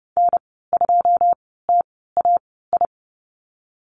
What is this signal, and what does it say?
Morse code, heard as a keyed tone, about 31 wpm, spaced as for 17 wpm, 700 Hz. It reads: N2TAI
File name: Lesson 19g - Letter w - Call Signs 30wpm